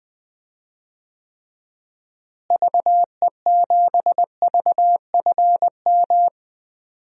Morse code, heard as a keyed tone, about 20 wpm, 700 Hz. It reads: VE7VFM
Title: DXCC-100 Call Signs - 6 Characters - Repeated 3x - 20wpm